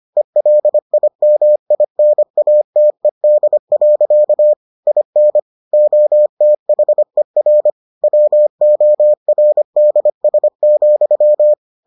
Morse code, heard as a keyed tone, about 25 wpm, 600 Hz. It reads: ELIMINATED. IN OTHER WORDS,